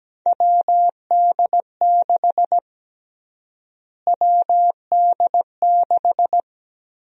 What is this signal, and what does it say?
Morse code, heard as a keyed tone, about 17 wpm, 700 Hz. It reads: WD6 WD6
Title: USA Call Sign Prefixes - Repeated 2x - 17wpm